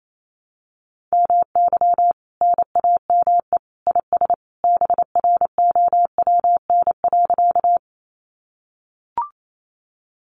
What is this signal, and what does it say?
Morse code, heard as a keyed tone, about 28 wpm, 700 Hz. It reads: MY NAME IS BROWN.
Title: Sentences from Top 1200 Words 28wpm